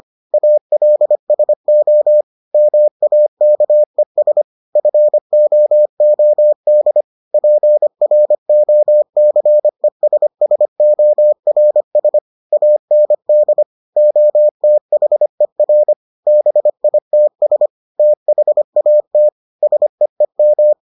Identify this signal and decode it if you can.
Morse code, heard as a keyed tone, about 25 wpm, 600 Hz. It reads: ALSO MAKES FOOD PROCESSORS AND OTHER BITS THAT SEEM